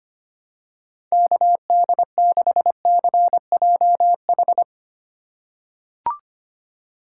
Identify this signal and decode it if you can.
Morse code, heard as a keyed tone, about 25 wpm, 700 Hz. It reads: KD6CJH
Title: USA Call Signs - 2x Character Spacing - 25wpm